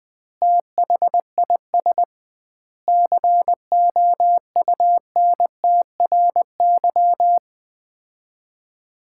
Morse code, heard as a keyed tone, about 20 wpm, 700 Hz. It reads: THIS COUNTRY
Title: 2x Word-Spacing - Sets of 3 Words - 20wpm